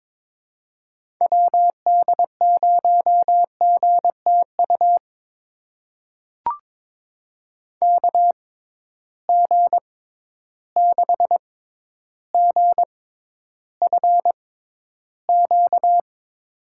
Morse code, heard as a keyed tone, about 21 wpm, spaced as for 11 wpm, 700 Hz. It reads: WD0GTU KG6GFQ
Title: USA Call Signs - 6 Characters - 6x Character Spacing - 22wpm